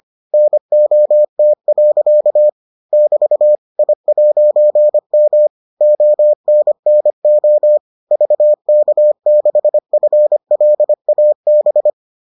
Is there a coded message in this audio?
Yes: Morse code from 0.3 to 11.9 s, heard as a keyed tone, about 25 wpm, 600 Hz. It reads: NOT. <BT> I'M ONNO VK6FLAB